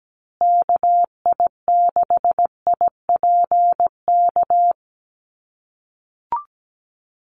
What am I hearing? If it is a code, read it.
Morse code, heard as a keyed tone, about 17 wpm, 700 Hz. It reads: KI6IPK
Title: USA Call Signs Repeated 2x 17wpm